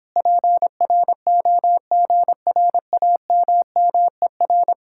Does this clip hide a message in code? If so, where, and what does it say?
Yes: Morse code from 0.2 to 4.7 s, heard as a keyed tone, about 26 wpm, 700 Hz. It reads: PROGRAMMER